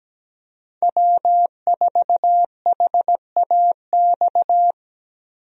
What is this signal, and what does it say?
Morse code, heard as a keyed tone, about 17 wpm, 700 Hz. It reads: W4HAX
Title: USA Call Signs - 5 Characters - Repeated 3x - 17wpm